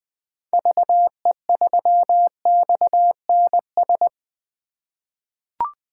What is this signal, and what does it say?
Morse code, heard as a keyed tone, about 20 wpm, 700 Hz. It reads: VE3XNS
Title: 2023 POTA Activator Call Signs - 20wpm